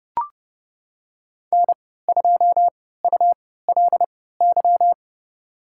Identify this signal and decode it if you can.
Morse code, heard as a keyed tone, about 30 wpm, spaced as for 17 wpm, 700 Hz. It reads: N2ULY
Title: USA Call Signs - 5 Characters - 3x Character Spacing - 30wpm